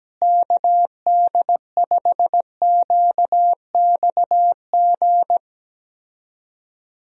Morse code, heard as a keyed tone, about 17 wpm, 700 Hz. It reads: KD5QXG
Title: USA Call Signs - 6 Characters - Repeated 3x - 17wpm